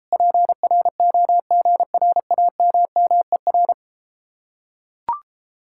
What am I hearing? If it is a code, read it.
Morse code, heard as a keyed tone, about 33 wpm, 700 Hz. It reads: PROGRAMMER